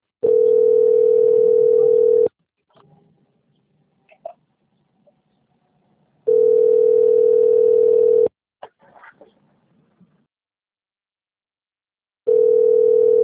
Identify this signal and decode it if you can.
Ringback tone (North America), 440 + 480 Hz.